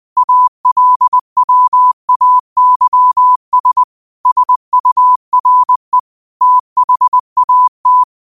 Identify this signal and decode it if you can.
Morse code, heard as a keyed tone, about 20 wpm, 1000 Hz. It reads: ALWAYS SURE THAT
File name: qotdfiles / 20wpm